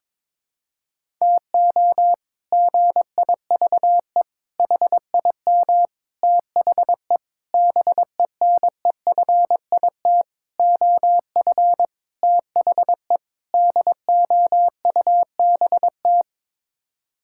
Morse code, heard as a keyed tone, about 22 wpm, 700 Hz. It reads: TO GIVE HIM THE BENEFIT OF THE DOUBT
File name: practice / 22wpm / Sets of 8 Words - 22wpm